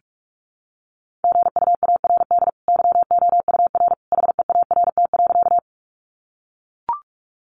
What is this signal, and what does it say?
Morse code, heard as a keyed tone, about 44 wpm, 700 Hz. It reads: GUARD YOUR HEART.
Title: Sentences from Top 1200 Words 45wpm